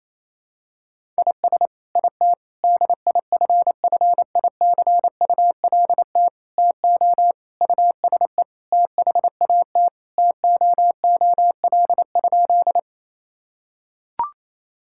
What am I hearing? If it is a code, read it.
Morse code, heard as a keyed tone, about 28 wpm, 700 Hz. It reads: IS IT DIFFICULT TO USE THAT TOOL?
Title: Sentences from Top 1000 Words - Review Part 6 - 28wpm